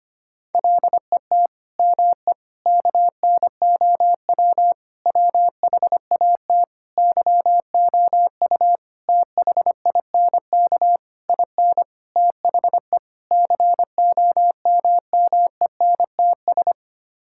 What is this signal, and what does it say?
Morse code, heard as a keyed tone, about 25 wpm, 700 Hz. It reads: LET ME KNOW WHAT YOU THINK IN THE COMMENTS
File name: Sets of 9 Words - 25wpm